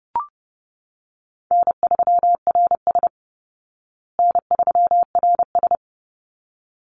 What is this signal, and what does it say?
Morse code, heard as a keyed tone, about 30 wpm, 700 Hz. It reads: N3RS N3RS
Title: Easy Call Signs Repeated 2x - 30wpm